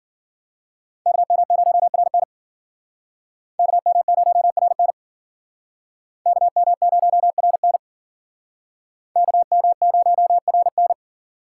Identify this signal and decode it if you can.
Morse code, heard as a keyed tone, about 52 wpm, 700 Hz. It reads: KM0RN KM0RN KM0RN KM0RN